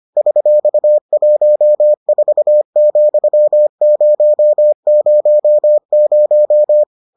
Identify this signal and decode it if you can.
Morse code, heard as a keyed tone, about 25 wpm, 600 Hz. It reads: $14,000